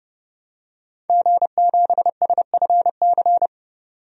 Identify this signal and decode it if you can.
Morse code, heard as a keyed tone, about 30 wpm, 700 Hz. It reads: G7SFC
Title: DXCC-100 Call Signs - 5 Characters - 30wpm